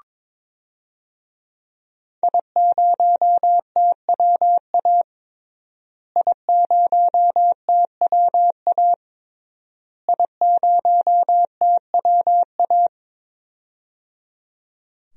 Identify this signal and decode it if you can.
Morse code, heard as a keyed tone, about 22 wpm, 700 Hz. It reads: I0TWA I0TWA I0TWA
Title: DXCC-100 Call Signs - 5 Characters - Repeated 3x - 22wpm